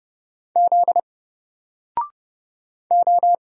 Morse code, heard as a keyed tone, about 31 wpm, 700 Hz. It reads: Z O
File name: Single-Letter-Number-30wpm